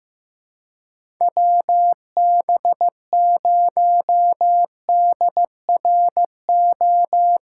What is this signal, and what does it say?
Morse code, heard as a keyed tone, about 15 wpm, 700 Hz. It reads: WB0DRO
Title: USA Call Signs - 6 Characters - 2x Character Spacing - 15wpm